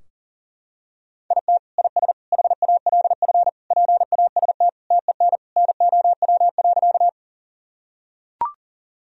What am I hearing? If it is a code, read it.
Morse code, heard as a keyed tone, about 40 wpm, 700 Hz. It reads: IT IS HALF PAST TEN NOW.
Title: Sentences from Top 1000 Words - Review Part 3 - 40wpm